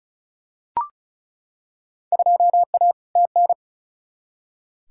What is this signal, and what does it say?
Morse code, heard as a keyed tone, about 36 wpm, 700 Hz. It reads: 2A TN
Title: Field Day Exchanges - 35wpm